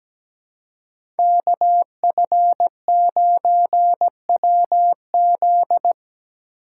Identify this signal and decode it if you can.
Morse code, heard as a keyed tone, about 17 wpm, 700 Hz. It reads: KF9WZ
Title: USA Call Signs - 5 Characters - 4x Character Spacing - 17wpm